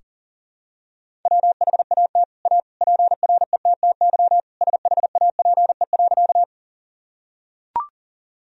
Morse code, heard as a keyed tone, about 37 wpm, 700 Hz. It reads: WHAT A PRETTY SHAPE.
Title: Sentences from Top 1000 Words - Review Part 3 - 40wpm